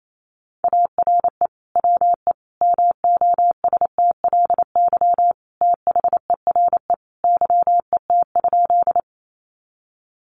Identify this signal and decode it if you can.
Morse code, heard as a keyed tone, about 28 wpm, 700 Hz. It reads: ARE WE MOSTLY THERE YET?
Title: Sentences from Top 1300 Words 28wpm